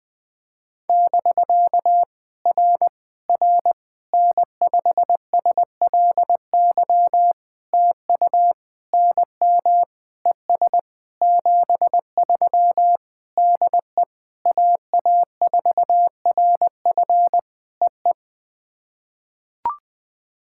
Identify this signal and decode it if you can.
Morse code, heard as a keyed tone, about 20 wpm, 700 Hz. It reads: <BK> R R N5SLY TU NM ES 73 DE AA4RF EE
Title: POTA QSO Practice - 20wpm